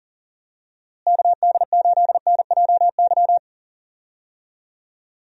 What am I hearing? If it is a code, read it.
Morse code, heard as a keyed tone, about 40 wpm, 700 Hz. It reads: KD8NJY
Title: USA Call Signs - 6 Characters - Repeated 2x - 40wpm